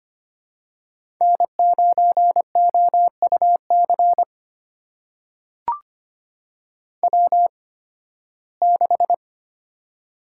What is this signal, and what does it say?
Morse code, heard as a keyed tone, about 25 wpm, spaced as for 16 wpm, 700 Hz. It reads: N9OUC W6
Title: USA Call Signs - 5 Characters - 8x Character Spacing - 25wpm